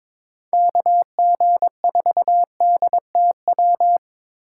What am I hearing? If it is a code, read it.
Morse code, heard as a keyed tone, about 22 wpm, 700 Hz. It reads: KG4DTW
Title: USA Call Signs - 3x Character Spacing - 22wpm